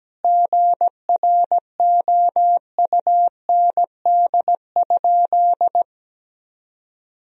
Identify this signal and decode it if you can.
Morse code, heard as a keyed tone, about 17 wpm, 700 Hz. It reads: GROUND?
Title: Sentences from Top 1000 Words - Review Part 2 - 17wpm